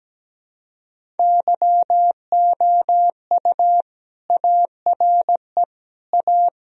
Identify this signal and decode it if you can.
Morse code, heard as a keyed tone, about 17 wpm, 700 Hz. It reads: YOU ARE A
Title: Sentences from Top 1000 Words - Review Part 1 - 17wpm